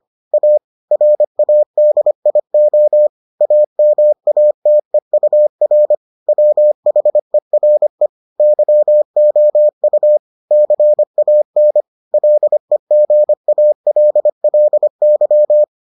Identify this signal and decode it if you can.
Morse code, heard as a keyed tone, about 25 wpm, 600 Hz. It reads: A RADIO AMATEUR WHERE YOU CAN LEGALLY